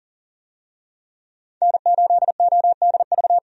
Morse code, heard as a keyed tone, about 40 wpm, 700 Hz. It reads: N8ODV
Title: USA Call Signs - 5 Characters - 40wpm